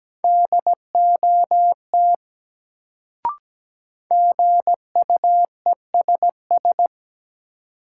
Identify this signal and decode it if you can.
Morse code, heard as a keyed tone, about 17 wpm, 700 Hz. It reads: DOT GUESS
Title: Sentences from Top 1500 Words - New Word Drill Alt - 17wpm